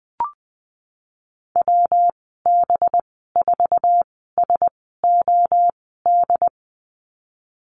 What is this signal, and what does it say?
Morse code, heard as a keyed tone, about 20 wpm, spaced as for 15 wpm, 700 Hz. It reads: WB4SOD
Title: 186 Lesson 30g - Letter b - Call Signs 20wpm